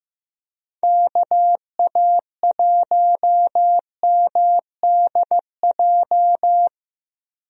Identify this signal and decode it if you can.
Morse code, heard as a keyed tone, about 15 wpm, 700 Hz. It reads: KA1MDJ
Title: USA Call Signs - 6 Characters - Repeated 3x - 15wpm